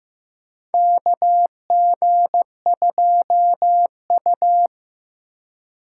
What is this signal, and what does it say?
Morse code, heard as a keyed tone, about 15 wpm, 700 Hz. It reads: KG2U
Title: USA Call Signs - 4 Characters - 2x Character Spacing - 15wpm